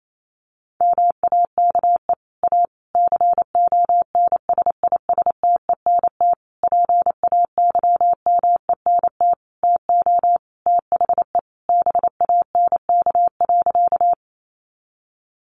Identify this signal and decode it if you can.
Morse code, heard as a keyed tone, about 28 wpm, 700 Hz. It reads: MAKE A CONSISTENT PAYMENT TO THE BANK.
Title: Sentences from Top 1500 Words 28wpm